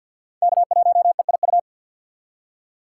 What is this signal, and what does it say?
Morse code, heard as a keyed tone, about 48 wpm, 700 Hz. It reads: K1EIU
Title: USA Call Signs - 2x Character Spacing - 50wpm